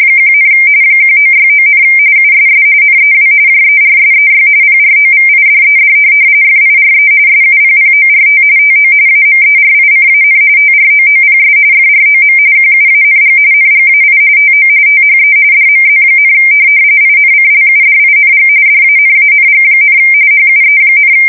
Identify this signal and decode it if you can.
RTTY, about 45.45 baud. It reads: 1, 1969, USS NEW JERSEY PROVIDED FIRE SUPPORT TO AMERICAN TROOPS FROM QUANG NGAI TO THE DMZ, INCLUDING DANANG, HUE,
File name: fsk170 / USS-New-Jersey